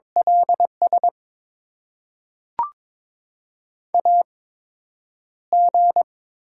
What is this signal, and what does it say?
Morse code, heard as a keyed tone, about 22 wpm, spaced as for 9 wpm, 700 Hz. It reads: LS AG
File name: Top 90 QSO Elements - 8x Character Spacing - 22wpm